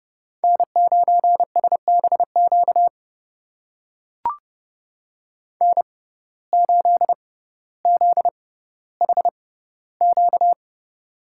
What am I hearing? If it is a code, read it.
Morse code, heard as a keyed tone, about 30 wpm, spaced as for 15 wpm, 700 Hz. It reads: N9SBQ N8ZHQ
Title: USA Call Signs - 5 Characters - 6x Character Spacing - 30wpm